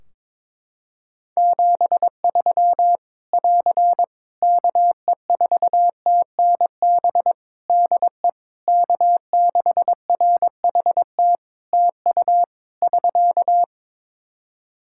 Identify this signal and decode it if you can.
Morse code, heard as a keyed tone, about 21 wpm, 700 Hz. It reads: 73 <AR> KE4TNB DE K6RHT TU <SK>
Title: Practice QSO - Exchange 4 - Ending - 22wpm